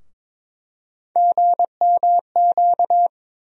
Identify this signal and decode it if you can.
Morse code, heard as a keyed tone, about 22 wpm, 700 Hz. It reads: GMQ
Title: USA Call Sign Postfixes - 3 Letters 22wpm